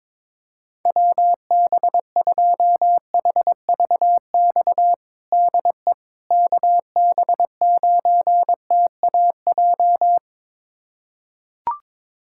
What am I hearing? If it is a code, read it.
Morse code, heard as a keyed tone, about 22 wpm, 700 Hz. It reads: WB2HVX DE KB9TAJ
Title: USA Callsign DE Callsign - 6 Characters - 22wpm